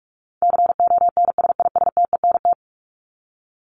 Morse code, heard as a keyed tone, about 44 wpm, 700 Hz. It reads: CONSISTENT
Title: Top 1251-1500 Words 45wpm